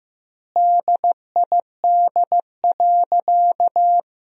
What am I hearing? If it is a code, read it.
Morse code, heard as a keyed tone, about 15 wpm, 700 Hz. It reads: DID.
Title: 2x Word-Spacing - Sentences from Top 500 Words - 15wpm